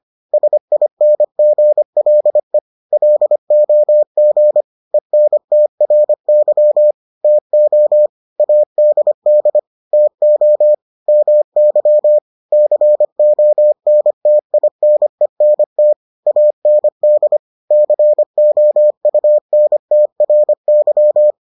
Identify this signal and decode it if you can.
Morse code, heard as a keyed tone, about 25 wpm, 600 Hz. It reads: SINGLE LOG ENTRY TO ADD TO MY CONTINENT AND COUNTRY